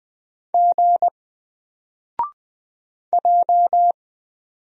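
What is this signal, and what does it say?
Morse code, heard as a keyed tone, about 20 wpm, 700 Hz. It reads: G J
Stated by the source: Single letter and number Morse drill at 20 wpm, from N4LCD.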